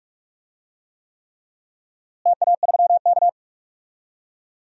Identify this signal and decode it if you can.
Morse code, heard as a keyed tone, about 46 wpm, 700 Hz. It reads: TA3K
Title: DXCC-100 Call Signs - 4 Characters - Repeated 3x - 45wpm